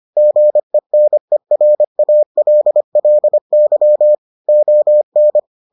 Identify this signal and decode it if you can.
Morse code, heard as a keyed tone, about 25 wpm, 600 Hz. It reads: GENERALLY ON